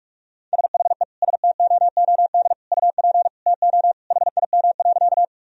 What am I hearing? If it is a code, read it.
Morse code, heard as a keyed tone, about 46 wpm, 700 Hz. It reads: SHE STOOD UP TO HIM.
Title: Sentences from Top 1000 Words - Review Part 2 - 45wpm